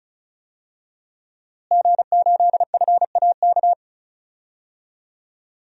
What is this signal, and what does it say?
Morse code, heard as a keyed tone, about 36 wpm, 700 Hz. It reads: G8FAK
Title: DXCC-100 Call Signs - 5 Characters - Repeated 2x - 35wpm